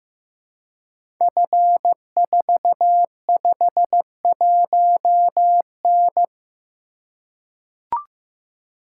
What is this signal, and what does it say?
Morse code, heard as a keyed tone, about 15 wpm, 700 Hz. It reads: F451N
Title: Code Groups - 5 Letter:Numbers - 15wpm